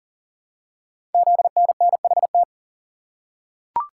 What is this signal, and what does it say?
Morse code, heard as a keyed tone, about 39 wpm, 700 Hz. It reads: ZNNHT
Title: Code Groups - 5 Letters - 40wpm